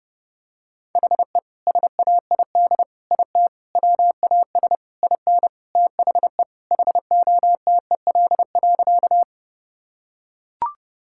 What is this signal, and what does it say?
Morse code, heard as a keyed tone, about 30 wpm, 700 Hz. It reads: HE SAID IT WAS IN THE HOTEL.
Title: Sentences from Top 1600 Words 30wpm